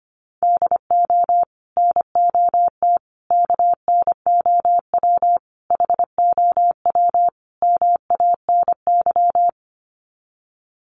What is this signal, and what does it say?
Morse code, heard as a keyed tone, about 25 wpm, 700 Hz. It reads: DO NOT KNOW HOW MANY
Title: Sets of 5 Words Encore 25wpm